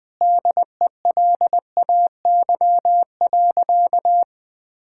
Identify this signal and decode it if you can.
Morse code, heard as a keyed tone, about 21 wpm, 700 Hz. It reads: DELAY.
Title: Sentences with 2-Word Phrases - Spoken Beginning - 20wpm